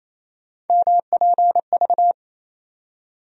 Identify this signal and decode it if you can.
Morse code, heard as a keyed tone, about 28 wpm, 700 Hz. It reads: MPV
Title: USA Call Sign Postfixes - 3 Letters 28wpm